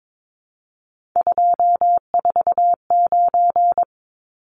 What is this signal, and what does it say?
Morse code, heard as a keyed tone, about 22 wpm, 700 Hz. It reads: 249
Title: Numbers - 3-Digit 22wpm